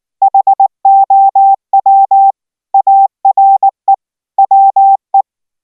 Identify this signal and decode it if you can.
Morse code, heard as a keyed tone, about 19 wpm, 780 Hz. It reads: HOW ARE WE